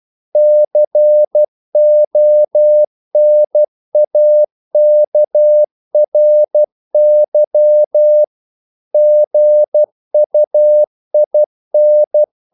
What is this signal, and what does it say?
Morse code, heard as a keyed tone, about 12 wpm, 600 Hz. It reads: CONAKRY GUIN
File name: CW / Practice / NationalCapitals12